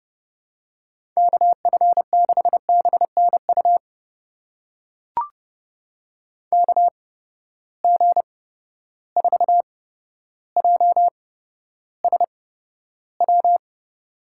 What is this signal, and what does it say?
Morse code, heard as a keyed tone, about 30 wpm, spaced as for 13 wpm, 700 Hz. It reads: KF6BNU KG4JSW
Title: USA Call Signs - 6 Characters - 8x Character Spacing - 30wpm